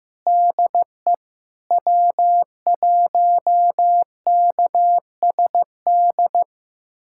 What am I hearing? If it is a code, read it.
Morse code, heard as a keyed tone, about 15 wpm, 700 Hz. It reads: DE W1KSD
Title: USA Callsign DE Callsign 15wpm